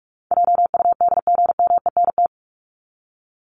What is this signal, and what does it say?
Morse code, heard as a keyed tone, about 44 wpm, 700 Hz. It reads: JUDGMENT
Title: Top 1751-2000 Words 45wpm